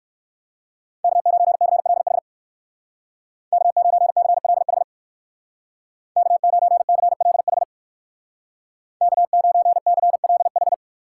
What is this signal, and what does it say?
Morse code, heard as a keyed tone, about 62 wpm, 700 Hz. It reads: K9CLH K9CLH K9CLH K9CLH